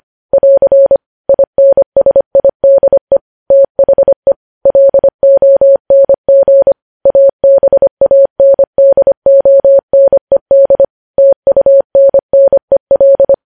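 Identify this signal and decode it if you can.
Morse code, heard as a keyed tone, about 25 wpm, 550 Hz. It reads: <AR> INSIDE THE LONG ABANDONED TUNNEL
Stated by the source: Morse code news headlines sent at 25 wpm.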